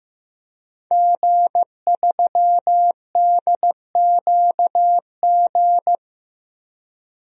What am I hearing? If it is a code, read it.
Morse code, heard as a keyed tone, about 15 wpm, 700 Hz. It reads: G3DQG
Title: DXCC-100 Call Signs - 5 Characters - Repeated 3x - 15wpm